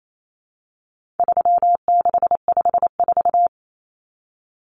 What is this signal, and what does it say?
Morse code, heard as a keyed tone, about 28 wpm, 700 Hz. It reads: 3654
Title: Common Cut Numbers - 4-Digit 28wpm